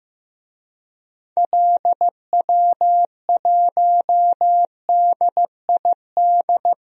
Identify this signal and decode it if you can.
Morse code, heard as a keyed tone, about 15 wpm, 700 Hz. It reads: LW1DID
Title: DXCC-100 Call Signs - 6 Characters - 15wpm